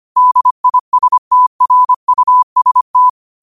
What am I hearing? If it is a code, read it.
Morse code, heard as a keyed tone, about 25 wpm, 1000 Hz. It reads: DISTRUST
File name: qotdfiles / 25wpm